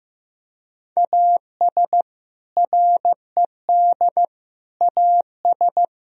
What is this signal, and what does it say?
Morse code, heard as a keyed tone, about 16 wpm, 700 Hz. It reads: AS RED AS
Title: Sentences from Top 1000 Words - Review Part 2 - 15wpm